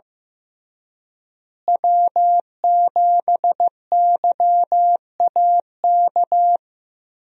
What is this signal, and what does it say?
Morse code, heard as a keyed tone, about 15 wpm, 700 Hz. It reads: W7YAK
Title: USA Call Signs - 5 Characters - Repeated 3x - 15wpm